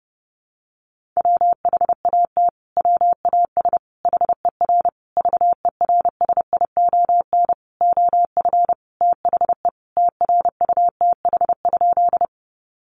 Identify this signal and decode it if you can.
Morse code, heard as a keyed tone, about 30 wpm, 700 Hz. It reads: WHAT WAS HER VERSION OF THE TRUTH?
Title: Sentences from Top 1400 Words 30wpm